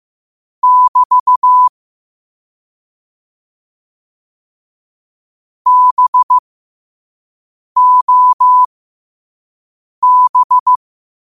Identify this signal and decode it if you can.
Morse code, heard as a keyed tone, about 15 wpm, spaced as for 5 wpm, 1000 Hz. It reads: <BT> BOB